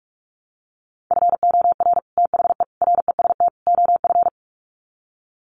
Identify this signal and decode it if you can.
Morse code, heard as a keyed tone, about 44 wpm, 700 Hz. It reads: FOR THE REST OF
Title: Sets of 4 Words 45wpm